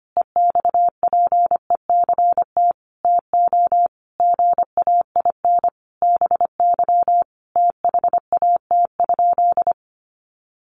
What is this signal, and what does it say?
Morse code, heard as a keyed tone, about 25 wpm, 700 Hz. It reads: EXPECT TO GAIN BY THAT?
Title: Sentences from Top 1100 Words 25wpm